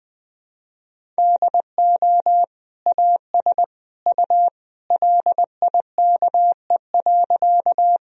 Morse code, heard as a keyed tone, about 21 wpm, 700 Hz. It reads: DO AS U LIKE.
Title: Sentences from Top 100 Words and Abbreviations 20wpm